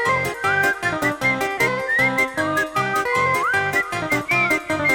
Emergency Alert System (SAME) tones, heard as harsh buzzing bursts. It is heard from 0.4 to 3.9 s.